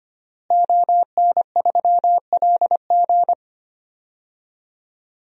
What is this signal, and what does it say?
Morse code, heard as a keyed tone, about 25 wpm, 700 Hz. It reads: ON3LG
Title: DXCC-100 Call Signs - 5 Characters - Repeated 2x - 25wpm